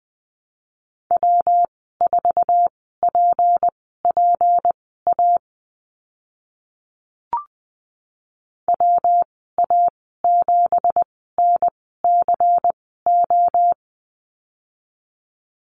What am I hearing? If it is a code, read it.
Morse code, heard as a keyed tone, about 20 wpm, spaced as for 14 wpm, 700 Hz. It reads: W4PPA WA7NCO
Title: 193 Lesson 31g - Letter v - Call Signs 20wpm